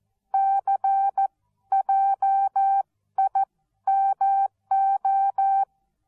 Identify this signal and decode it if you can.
Morse code, heard as a keyed tone, about 14 wpm, spaced as for 12 wpm, 780 Hz. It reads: CJIMO
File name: Onlinelehrgang / morsen / DJ4UF Morsen13